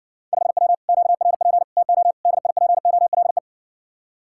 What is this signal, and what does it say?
Morse code, heard as a keyed tone, about 60 wpm, 700 Hz. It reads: 5W QRP TO DIPOLE